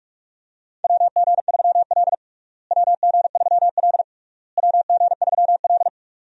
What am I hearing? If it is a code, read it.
Morse code, heard as a keyed tone, about 45 wpm, 700 Hz. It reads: WG3L WG3L WG3L
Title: USA Call Signs - 4 Characters - Repeated 3x - 45wpm